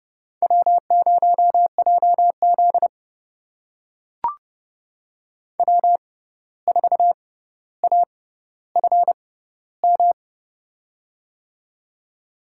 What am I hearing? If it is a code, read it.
Morse code, heard as a keyed tone, about 30 wpm, spaced as for 14 wpm, 700 Hz. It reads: W0JZ W4AFM
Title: USA Call Signs - 6x Character Spacing - 30wpm